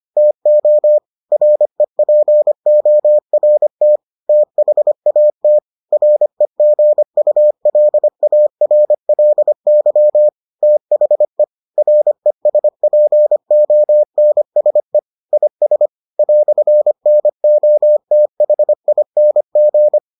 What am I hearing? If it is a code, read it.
Morse code, heard as a keyed tone, about 25 wpm, 600 Hz. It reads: TO REPORT THAT REGULARLY THE RESPONSE IS "NOTHING